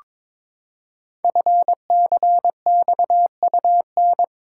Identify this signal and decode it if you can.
Morse code, heard as a keyed tone, about 22 wpm, 700 Hz. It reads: FCXUN
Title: Code Groups - 5 Letters - 22wpm